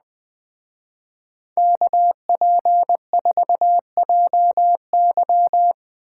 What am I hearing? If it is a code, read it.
Morse code, heard as a keyed tone, about 19 wpm, 700 Hz. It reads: KP4JY